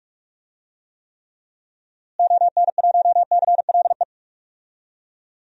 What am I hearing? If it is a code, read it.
Morse code, heard as a keyed tone, about 46 wpm, 700 Hz. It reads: ON1CLE
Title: DXCC-100 Call Signs - 6 Characters - Repeated 2x - 45wpm